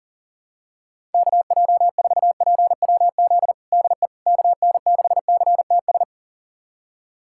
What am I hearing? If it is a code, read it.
Morse code, heard as a keyed tone, about 37 wpm, 700 Hz. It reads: KJ4PWZ DE KN6CTS
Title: USA Callsign DE Callsign 40wpm